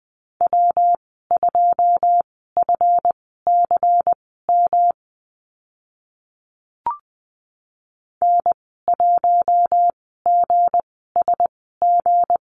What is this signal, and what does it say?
Morse code, heard as a keyed tone, about 20 wpm, spaced as for 14 wpm, 700 Hz. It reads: W2FCM N1GSG
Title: 161 Lesson 26g - Letter g - Call Signs 20wpm